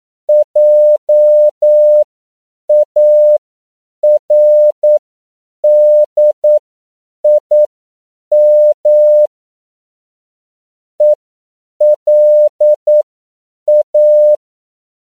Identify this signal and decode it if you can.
Morse code, heard as a keyed tone, about 9 wpm, spaced as for 7 wpm, 600 Hz. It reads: JARDIM ELA
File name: CW / 10PPM